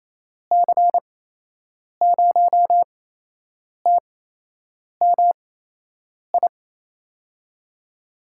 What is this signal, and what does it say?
Morse code, heard as a keyed tone, about 28 wpm, spaced as for 8 wpm, 700 Hz. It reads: C0TMI
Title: USA Call Signs - 8x Character Spacing - 28wpm